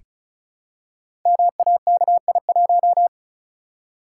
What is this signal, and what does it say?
Morse code, heard as a keyed tone, about 36 wpm, 700 Hz. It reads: MAKI1
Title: Code Groups - 5 Letter:Numbers - 35wpm